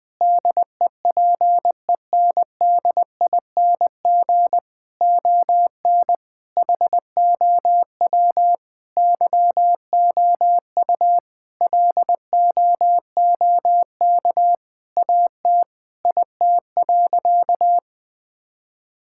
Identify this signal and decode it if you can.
Morse code, heard as a keyed tone, about 20 wpm, 700 Hz. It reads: DEPENDING ON HOW YOU LOOK AT IT.
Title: Sentences with 7-Word Phrases - 20wpm